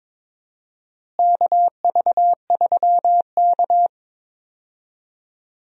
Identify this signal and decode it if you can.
Morse code, heard as a keyed tone, about 22 wpm, 700 Hz. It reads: KV3K
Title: USA Call Signs - 4 Characters - Repeated 2x - 22wpm